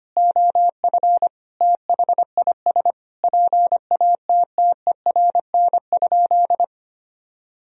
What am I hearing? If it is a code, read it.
Morse code, heard as a keyed tone, about 25 wpm, 700 Hz. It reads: OF THIS PATTERN?
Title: Sentences from Top 1000 Words - Review Part 2 - 25wpm